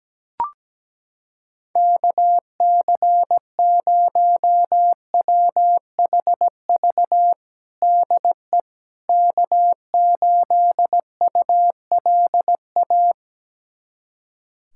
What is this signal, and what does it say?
Morse code, heard as a keyed tone, about 17 wpm, 700 Hz. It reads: KC0WHV DE K8ULA
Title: USA Callsign DE Callsign 17wpm